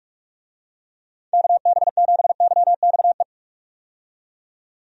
Morse code, heard as a keyed tone, about 46 wpm, 700 Hz. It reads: KB7YXE
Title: USA Call Signs - 6 Characters - 45wpm